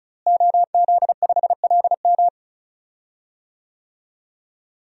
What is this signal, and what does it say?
Morse code, heard as a keyed tone, about 35 wpm, 700 Hz. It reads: OZ5LM
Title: DXCC-100 Call Signs - 5 Characters - Repeated 3x - 35wpm